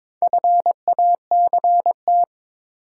Morse code, heard as a keyed tone, about 22 wpm, 700 Hz. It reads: FACT
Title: Sentences from Top 500 Words - New Word Drill - 22wpm